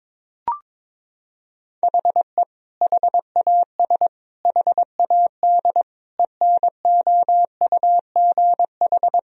Morse code, heard as a keyed tone, about 22 wpm, 700 Hz. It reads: HE HAS HAD ENOUGH
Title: Sentences from Top 1000 Words - Review Part 3 - 22wpm